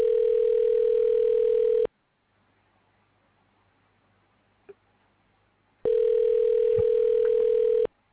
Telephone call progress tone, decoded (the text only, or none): ringback tone